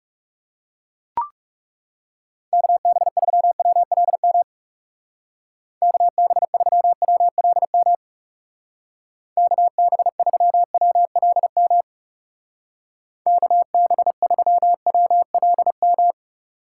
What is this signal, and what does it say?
Morse code, heard as a keyed tone, about 38 wpm, 700 Hz. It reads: KB3WLM KB3WLM KB3WLM KB3WLM